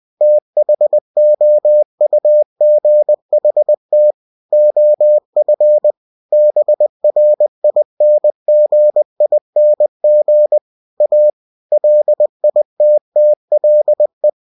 Morse code, heard as a keyed tone, about 20 wpm, 600 Hz. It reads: THOUGHT OF BRINGING A LITTLE